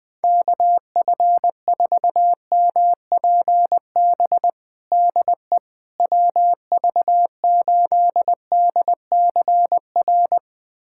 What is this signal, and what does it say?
Morse code, heard as a keyed tone, about 21 wpm, 690 Hz. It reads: KF4MPB DE WV8DCR
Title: USA Callsign DE Callsign 20wpm